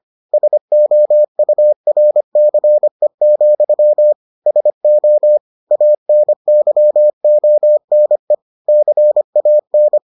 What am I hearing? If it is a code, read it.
Morse code, heard as a keyed tone, about 25 wpm, 600 Hz. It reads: SOURCE, SO ANYONE CAN